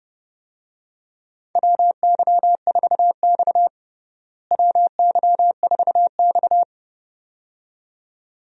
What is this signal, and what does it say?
Morse code, heard as a keyed tone, about 29 wpm, 700 Hz. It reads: WY4X WY4X
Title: USA Call Signs - 4 Characters - Repeated 2x - 30wpm